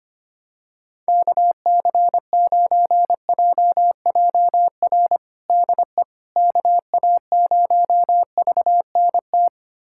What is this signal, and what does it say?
Morse code, heard as a keyed tone, about 25 wpm, 700 Hz. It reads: KC9JJR DE KA0VNT
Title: USA Callsign DE Callsign - 6 Characters - 25wpm